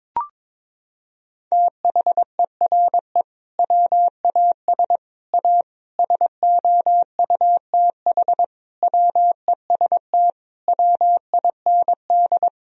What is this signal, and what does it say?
Morse code, heard as a keyed tone, about 22 wpm, 700 Hz. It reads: THERE WAS A SOUTH WEST WIND
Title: Sentences from Top 1000 Words - Review Part 2 - 22wpm